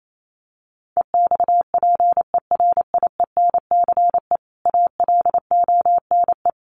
Morse code, heard as a keyed tone, about 28 wpm, 700 Hz. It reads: EXPERIENCE ALONE
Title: Sentences from Top 1000 Words 28wpm